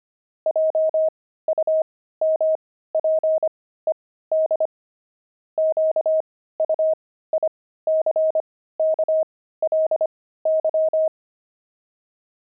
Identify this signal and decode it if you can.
Morse code, heard as a keyed tone, about 25 wpm, spaced as for 15 wpm, 630 Hz. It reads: JUMPED QUICKLY
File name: pangram-fs-15-cs-25-jumping